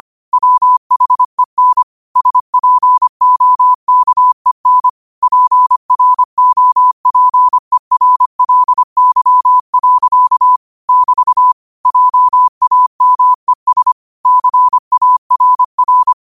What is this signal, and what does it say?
Morse code, heard as a keyed tone, about 25 wpm, 1000 Hz. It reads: WHEN SPOKEN PROPERLY. <BT> JAMES CARR